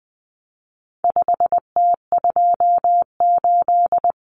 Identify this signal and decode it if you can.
Morse code, heard as a keyed tone, about 20 wpm, 700 Hz. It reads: 5T28
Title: Common Cut Numbers - 4-Digit 20wpm